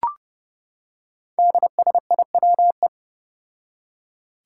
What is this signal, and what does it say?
Morse code, heard as a keyed tone, about 28 wpm, 700 Hz. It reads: DSIWE
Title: Five Challenging Letter-Number Combinations 30wpm